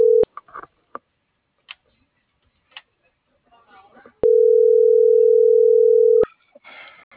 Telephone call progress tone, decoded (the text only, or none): ringback tone